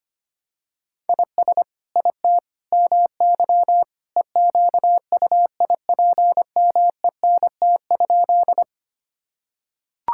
Morse code, heard as a keyed tone, about 25 wpm, 700 Hz. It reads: IS IT MY EQUIPMENT?
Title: Sentences from Top 1000 Words - Review Part 5 - 25wpm